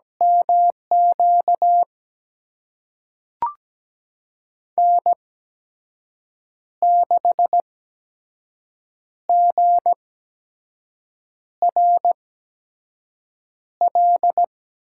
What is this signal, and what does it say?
Morse code, heard as a keyed tone, about 17 wpm, spaced as for 5 wpm, 700 Hz. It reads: MQ N6GRL
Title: USA Call Signs - 8x Character Spacing - 17wpm